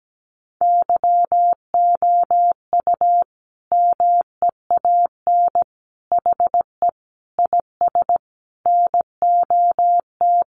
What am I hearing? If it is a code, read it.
Morse code, heard as a keyed tone, about 17 wpm, 700 Hz. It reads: YOU MEAN HE IS NOT
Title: Sentences from Top 1200 Words 17wpm